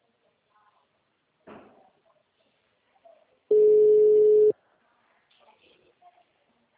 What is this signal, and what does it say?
Ringback tone (Europe, 425 Hz), with speech.